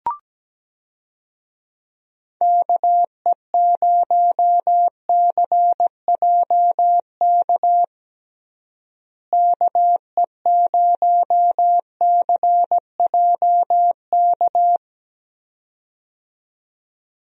Morse code, heard as a keyed tone, about 17 wpm, 700 Hz. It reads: KE0CJK KE0CJK
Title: USA Call Signs - 6 Characters - Repeated 2x - 17wpm